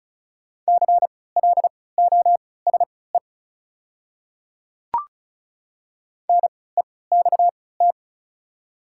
Morse code, heard as a keyed tone, about 36 wpm, spaced as for 20 wpm, 700 Hz. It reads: CLOSE NEXT
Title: 3x Character Spacing - Top 300 Words - 35wpm